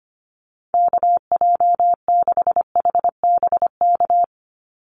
Morse code, heard as a keyed tone, about 25 wpm, 700 Hz. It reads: KJ6HBK
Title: USA Call Signs 25wpm